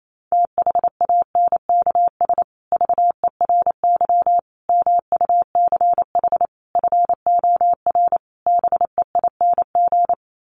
Morse code, heard as a keyed tone, about 28 wpm, 700 Hz. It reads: THANKS VERY MUCH FOR BEING
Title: Sets of 5 Words Encore 28wpm